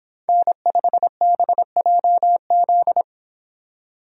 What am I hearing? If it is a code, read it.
Morse code, heard as a keyed tone, about 31 wpm, 700 Hz. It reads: N5BJZ